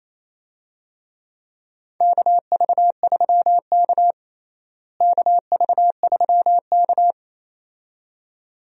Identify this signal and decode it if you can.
Morse code, heard as a keyed tone, about 28 wpm, 700 Hz. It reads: KV3K KV3K
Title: USA Call Signs - 4 Characters - Repeated 2x - 28wpm